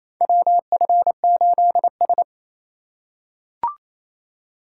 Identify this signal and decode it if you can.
Morse code, heard as a keyed tone, about 28 wpm, 700 Hz. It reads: WF8S
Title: USA Call Signs - 4 Characters - 4x Character Spacing - 28wpm